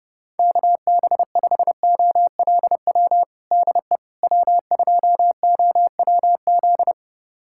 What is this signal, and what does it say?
Morse code, heard as a keyed tone, about 30 wpm, 700 Hz. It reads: KB5OLW DE W2OWZ
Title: USA Callsign DE Callsign 30wpm